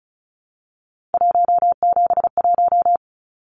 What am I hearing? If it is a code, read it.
Morse code, heard as a keyed tone, about 34 wpm, 700 Hz. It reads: 171
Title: Numbers - 3-Digit 35wpm